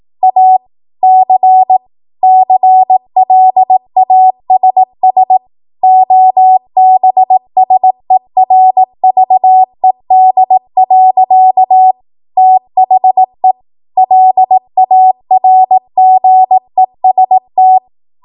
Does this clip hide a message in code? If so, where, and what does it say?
Yes: Morse code from 0.2 to 17.8 s, heard as a keyed tone, about 18 wpm, 750 Hz. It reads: A C CLASS OBSERVED. THE LARGEST